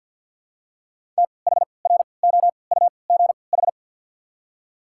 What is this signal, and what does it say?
Morse code, heard as a keyed tone, about 48 wpm, spaced as for 28 wpm, 690 Hz. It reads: THROUGH
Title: 3x Character Spacing - Top 500 Words - 50wpm